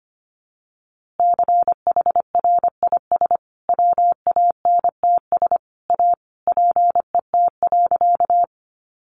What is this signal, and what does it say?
Morse code, heard as a keyed tone, about 25 wpm, 700 Hz. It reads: CHRIS WANTS A PET.
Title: Easy Phrases 25wpm